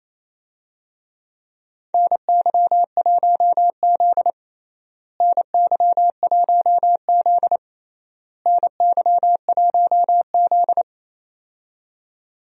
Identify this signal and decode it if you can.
Morse code, heard as a keyed tone, about 27 wpm, 700 Hz. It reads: NY1Z NY1Z NY1Z
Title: USA Call Signs - 4 Characters - Repeated 3x - 28wpm